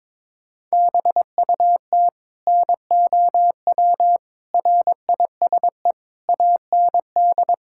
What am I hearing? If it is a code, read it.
Morse code, heard as a keyed tone, about 22 wpm, 700 Hz. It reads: BUT NOW RISE AND
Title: Sentences from Top 1000 Words - Review Part 6 - 22wpm